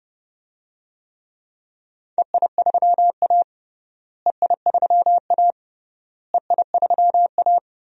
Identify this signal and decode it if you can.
Morse code, heard as a keyed tone, about 29 wpm, 700 Hz. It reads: EI3A EI3A EI3A
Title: DXCC-100 Call Signs - 4 Characters - Repeated 3x - 30wpm